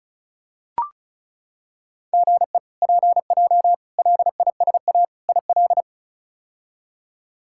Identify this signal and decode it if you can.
Morse code, heard as a keyed tone, about 36 wpm, 700 Hz. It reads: GE PJ LISA IL
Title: K1USN SST QSO Practice - 35wpm